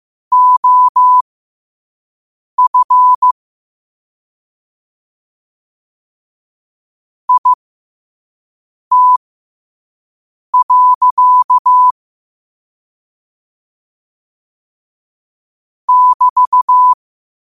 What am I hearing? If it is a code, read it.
Morse code, heard as a keyed tone, about 15 wpm, spaced as for 5 wpm, 1000 Hz. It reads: OF IT. <BT>
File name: qotdfiles / 5wpm